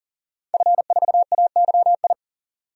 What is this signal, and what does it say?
Morse code, heard as a keyed tone, about 40 wpm, 700 Hz. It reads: F4AYI
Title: DXCC-100 Call Signs - 5 Characters - Repeated 2x - 40wpm